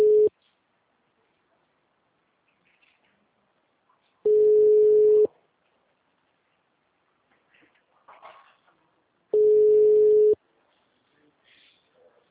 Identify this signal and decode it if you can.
Ringback tone (Europe, 425 Hz).